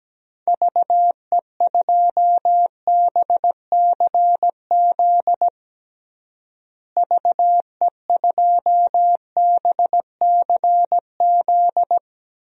Morse code, heard as a keyed tone, about 17 wpm, 700 Hz. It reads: VE2BCZ VE2BCZ
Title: DXCC-100 Call Signs - 6 Characters - Repeated 2x - 17wpm